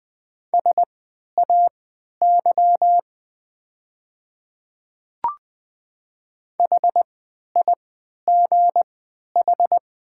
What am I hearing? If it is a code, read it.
Morse code, heard as a keyed tone, about 20 wpm, spaced as for 11 wpm, 690 Hz. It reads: SAY HIGH
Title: 3x Character Spacing - Top 300 Words - 20wpm